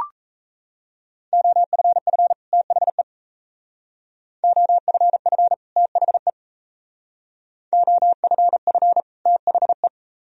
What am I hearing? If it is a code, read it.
Morse code, heard as a keyed tone, about 36 wpm, 700 Hz. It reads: OFF THE OFF THE OFF THE